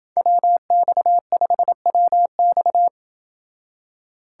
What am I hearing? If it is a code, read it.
Morse code, heard as a keyed tone, about 28 wpm, 700 Hz. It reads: WX5WX